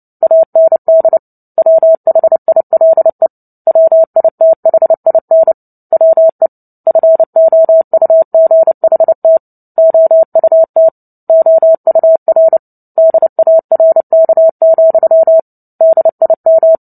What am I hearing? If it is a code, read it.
Morse code, heard as a keyed tone, about 30 wpm, 650 Hz. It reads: AND WHILE WITHIN WE FOUGHT OUT OUR DARK, DIM
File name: Book Two / 03-Chapter 3 - 29 WPM